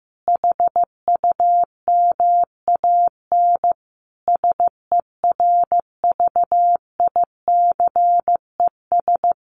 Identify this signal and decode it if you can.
Morse code, heard as a keyed tone, about 15 wpm, 700 Hz. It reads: HUMAN SERVICES
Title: Sets of 5 Words 15wpm